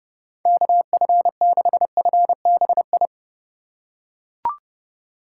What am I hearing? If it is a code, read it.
Morse code, heard as a keyed tone, about 30 wpm, 700 Hz. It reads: KF6FBI
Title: USA Call Signs - 6 Characters - 3x Character Spacing - 30wpm